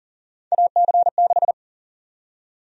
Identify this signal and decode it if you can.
Morse code, heard as a keyed tone, about 40 wpm, 700 Hz. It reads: AC6
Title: USA Call Sign Prefixes - Repeated 2x - 40wpm